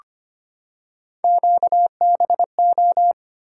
Morse code, heard as a keyed tone, about 25 wpm, 700 Hz. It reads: QBO
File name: Three Letter-Number Combinations - 25wpm